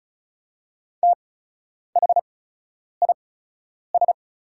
Morse code, heard as a keyed tone, about 36 wpm, spaced as for 10 wpm, 690 Hz. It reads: THIS
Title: 8x Character Spacing - Top 300 Words - 35wpm